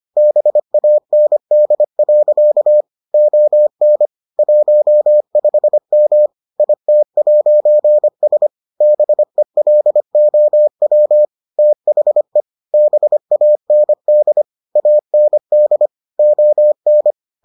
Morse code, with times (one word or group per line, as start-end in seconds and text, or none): 0.2-2.8 BAND.
3.1-4.1 ON
4.4-6.3 15M
6.6-8.5 IT'S
8.8-11.2 BELOW
11.6-12.4 THE
12.7-14.4 BAND
14.8-15.9 AND
16.2-17.1 ON